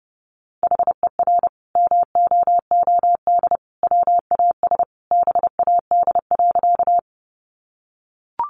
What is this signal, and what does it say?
Morse code, heard as a keyed tone, about 30 wpm, 700 Hz. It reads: HER MOOD WAS BAD.
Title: Sentences from Top 1100 Words 30wpm